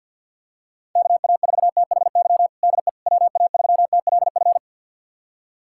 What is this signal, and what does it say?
Morse code, heard as a keyed tone, about 50 wpm, 700 Hz. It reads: KA4THY DE WA3TLF
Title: USA Callsign DE Callsign 50wpm